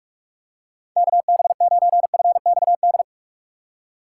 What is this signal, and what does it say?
Morse code, heard as a keyed tone, about 41 wpm, 700 Hz. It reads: KB9FXD
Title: USA Call Signs - 6 Characters - 4x Character Spacing - 45wpm